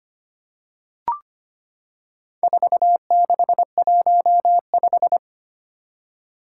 Morse code, heard as a keyed tone, about 25 wpm, 700 Hz. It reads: 4615
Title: Four Challenging Letter-Number Combinations 25wpm